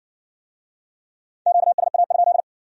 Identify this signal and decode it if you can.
Morse code, heard as a keyed tone, about 53 wpm, 700 Hz. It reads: QSA?